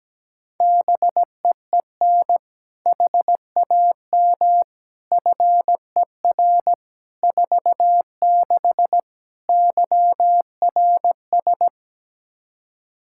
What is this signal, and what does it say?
Morse code, heard as a keyed tone, about 17 wpm, 700 Hz. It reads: BEEN HAM FER 46 YRS
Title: Practice QSO - Exchange 3 - More Info - 17wpm